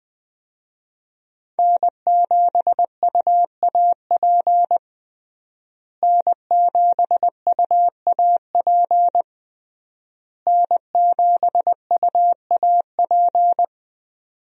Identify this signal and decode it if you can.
Morse code, heard as a keyed tone, about 20 wpm, 700 Hz. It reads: N7UAP N7UAP N7UAP
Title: USA Call Signs - 5 Characters - Repeated 3x - 20wpm